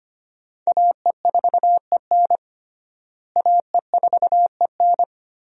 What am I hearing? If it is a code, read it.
Morse code, heard as a keyed tone, about 25 wpm, 700 Hz. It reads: AE4EN AE4EN
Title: USA Call Signs - 5 Characters - Repeated 3x - 25wpm